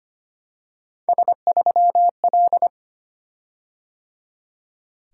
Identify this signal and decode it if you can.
Morse code, heard as a keyed tone, about 25 wpm, 700 Hz. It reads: S3L
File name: Three Letter-Number Combinations - 25wpm